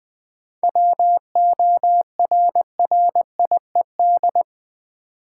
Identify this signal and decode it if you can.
Morse code, heard as a keyed tone, about 20 wpm, 690 Hz. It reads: WORRIED
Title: 7 Letter Words - First 3 Letters Given - 20wpm